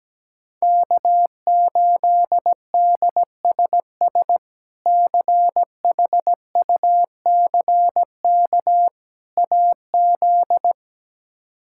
Morse code, heard as a keyed tone, about 17 wpm, 700 Hz. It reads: K8DSS CHUCK AZ
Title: K1USN SST QSO Practice - 17wpm